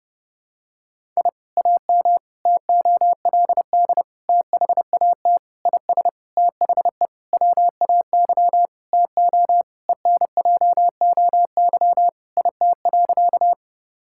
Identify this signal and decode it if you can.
Morse code, heard as a keyed tone, about 30 wpm, 700 Hz. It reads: I AM TOLD THAT IS THE WAY TO ENJOY IT.
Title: Sentences from Top 1000 Words - Review Part 5 - 30wpm